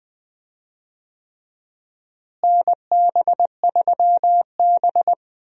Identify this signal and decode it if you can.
Morse code, heard as a keyed tone, about 20 wpm, 700 Hz. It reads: NB3B
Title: USA Call Signs - 4 Characters - Repeated 3x - 20wpm